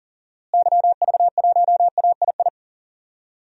Morse code, heard as a keyed tone, about 40 wpm, 700 Hz. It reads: YV1AII
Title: DXCC-100 Call Signs - 6 Characters - Repeated 2x - 40wpm